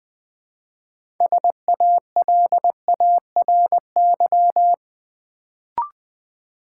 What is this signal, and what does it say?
Morse code, heard as a keyed tone, about 20 wpm, 700 Hz. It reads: SALARY
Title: Build A Word - 3 Letters At A Time - 6-Letter Words - 20wpm